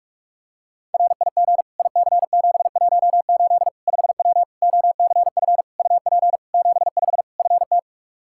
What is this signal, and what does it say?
Morse code, heard as a keyed tone, about 46 wpm, 700 Hz. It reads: RIG IC718 5W OCF UP 75 FT
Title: K1USN SST QSO Practice 45wpm